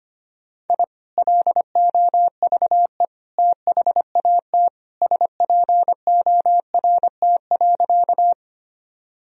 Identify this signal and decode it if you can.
Morse code, heard as a keyed tone, about 25 wpm, 700 Hz. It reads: I LOVE THAT SPORT.
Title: Sentences from Top 1000 Words - Review Part 4 - 25wpm